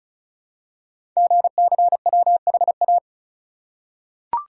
Morse code, heard as a keyed tone, about 36 wpm, 700 Hz. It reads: GCWHA
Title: Code Groups - 5 Letter:Numbers - 35wpm